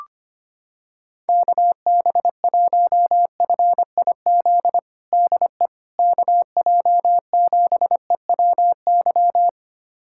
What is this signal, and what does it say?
Morse code, heard as a keyed tone, about 25 wpm, 700 Hz. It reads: KB1FIZ DE KJ7EWY
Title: USA Callsign DE Callsign 25wpm